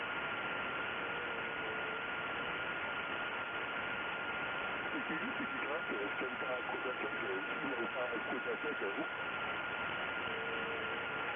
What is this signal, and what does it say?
FT8, with speech.